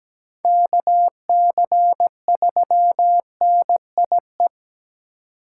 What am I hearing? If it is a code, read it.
Morse code, heard as a keyed tone, about 17 wpm, 700 Hz. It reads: KC3NIE
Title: K1USN SST QSO Practice - 17wpm